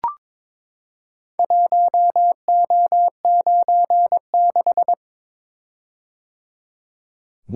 Morse code, heard as a keyed tone, about 22 wpm, 700 Hz. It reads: 1O96
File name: practice / 22wpm / Four Letter-Number Combinations - 22wpm